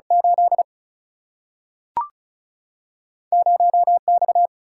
Morse code, heard as a keyed tone, about 36 wpm, 700 Hz. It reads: 8 0 X
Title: Single-Letter Single-Number 35wpm